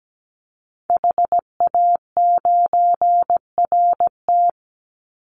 Morse code, heard as a keyed tone, about 17 wpm, 700 Hz. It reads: HA9RT
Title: Call Signs Repeated 3x - 17wpm